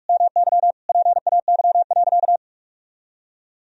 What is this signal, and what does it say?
Morse code, heard as a keyed tone, about 41 wpm, 700 Hz. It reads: MY PAY.
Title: Sentences from Top 1000 Words - Review Part 4 - 45wpm